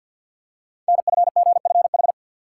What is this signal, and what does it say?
Morse code, heard as a keyed tone, about 51 wpm, 700 Hz. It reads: NFGFH
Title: Code Groups - 5 Letters - 50wpm